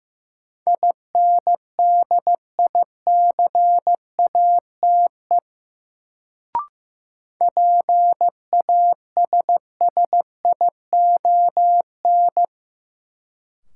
Morse code, heard as a keyed tone, about 15 wpm, 700 Hz. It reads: INDICATE PASSION
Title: Sentences from Top 1300 Words - New Word Drill - 15wpm